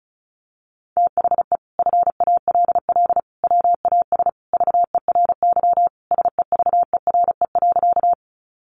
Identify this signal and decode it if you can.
Morse code, heard as a keyed tone, about 34 wpm, 700 Hz. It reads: THE FALL WAS VERY SEVERE.
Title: Sentences from Top 1400 Words 35wpm